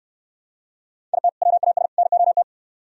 Morse code, heard as a keyed tone, about 66 wpm, 700 Hz. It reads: IT WAS NOT